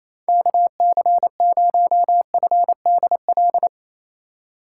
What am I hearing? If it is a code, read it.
Morse code, heard as a keyed tone, about 28 wpm, 700 Hz. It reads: KC0FDL
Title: USA Call Signs - 6 Characters - 2x Character Spacing - 28wpm